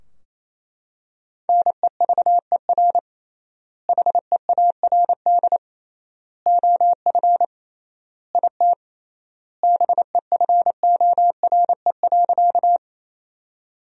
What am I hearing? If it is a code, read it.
Morse code, heard as a keyed tone, about 28 wpm, 700 Hz. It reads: NEVER HEARD OF IT BEFORE.
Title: 3x Word-Spacing - Sentences from Top 400 Words - 28wpm